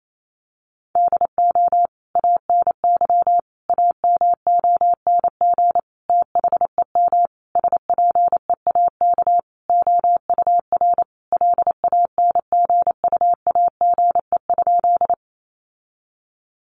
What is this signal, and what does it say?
Morse code, heard as a keyed tone, about 28 wpm, 700 Hz. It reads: DO ANY AMONG THEM SPEAK OUR LANGUAGE?
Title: Sentences from Top 1100 Words 28wpm